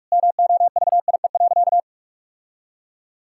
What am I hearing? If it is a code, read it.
Morse code, heard as a keyed tone, about 41 wpm, 700 Hz. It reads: MOVIE.
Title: Sentences from Top 1000 Words - Review Part 5 - 45wpm